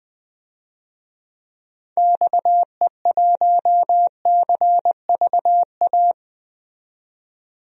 Morse code, heard as a keyed tone, about 20 wpm, 700 Hz. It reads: XE1CVA
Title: DXCC-100 Call Signs - 6 Characters - 20wpm